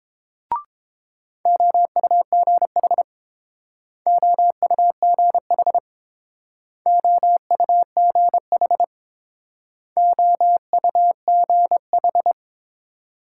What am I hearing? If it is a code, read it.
Morse code, heard as a keyed tone, about 29 wpm, 700 Hz. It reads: OUGH OUGH OUGH OUGH